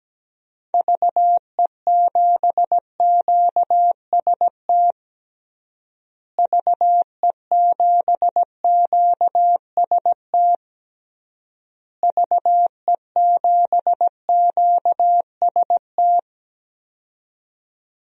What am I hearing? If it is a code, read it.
Morse code, heard as a keyed tone, about 17 wpm, 700 Hz. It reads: VE7QST VE7QST VE7QST
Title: DXCC-100 Call Signs - 6 Characters - Repeated 3x - 17wpm